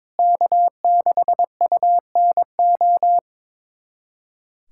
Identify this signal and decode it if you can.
Morse code, heard as a keyed tone, about 22 wpm, 700 Hz. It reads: K6UNO
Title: USA Callsign DE Callsign - 5 Characters - 22wpm